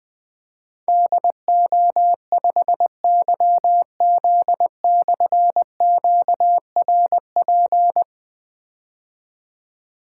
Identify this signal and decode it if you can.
Morse code, heard as a keyed tone, about 21 wpm, 700 Hz. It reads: DO5YZ/QRP
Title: Difficult Call Signs 20wpm